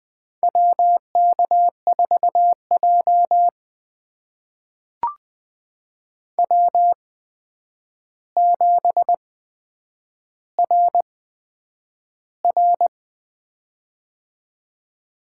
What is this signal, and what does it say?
Morse code, heard as a keyed tone, about 20 wpm, spaced as for 9 wpm, 700 Hz. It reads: WK4J W7RR
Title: USA Call Signs - 4 Characters - 8x Character Spacing - 20wpm